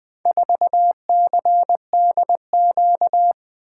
Morse code, heard as a keyed tone, about 20 wpm, 700 Hz. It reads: 4CDQ
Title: Four Letter-Number Combinations - 20wpm